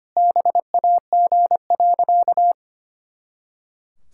Morse code, heard as a keyed tone, about 25 wpm, 700 Hz. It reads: BAG.
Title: Pangram Challenge v2 - 25wpm